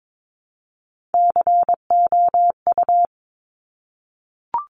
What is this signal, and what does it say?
Morse code, heard as a keyed tone, about 22 wpm, 700 Hz. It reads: COU
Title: Top Three Letter Combinations in Words 22wpm